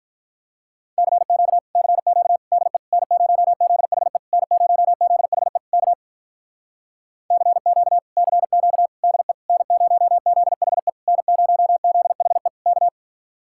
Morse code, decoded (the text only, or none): CQ CQ DE N0ZHE N0ZHE K CQ CQ DE N0ZHE N0ZHE K